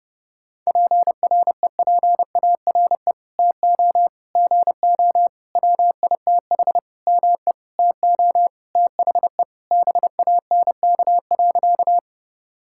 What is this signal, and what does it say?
Morse code, heard as a keyed tone, about 32 wpm, 700 Hz. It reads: PREPARE TO GO WITH ME TO THE BANK.
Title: Sentences from Top 1000 Words - Review Part 6 - 30wpm